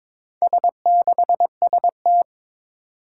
Morse code, heard as a keyed tone, about 23 wpm, 700 Hz. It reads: S6ST
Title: Four Challenging Letter-Number Combinations 22wpm